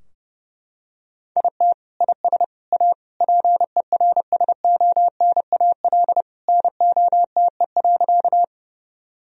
Morse code, heard as a keyed tone, about 32 wpm, 700 Hz. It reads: IT IS A PERSONAL NOTE.
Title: Sentences from Top 1000 Words - Review Part 4 - 30wpm